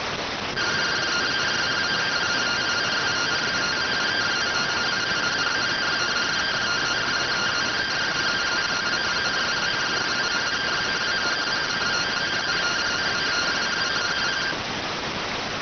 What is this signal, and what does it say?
Two-tone teleprinter signal, about 170 Hz shift.